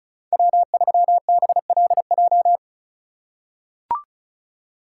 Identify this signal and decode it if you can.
Morse code, heard as a keyed tone, about 34 wpm, 700 Hz. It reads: W3BLJ
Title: USA Call Signs - 5 Characters - 3x Character Spacing - 35wpm